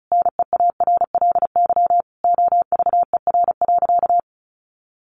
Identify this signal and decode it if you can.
Morse code, heard as a keyed tone, about 34 wpm, 700 Hz. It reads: NEARLY OVER.
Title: Sentences from Top 1500 Words 35wpm